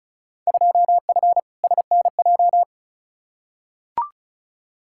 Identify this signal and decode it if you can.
Morse code, heard as a keyed tone, about 34 wpm, 700 Hz. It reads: 2F SNJ
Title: Field Day Exchanges Encore - 35wpm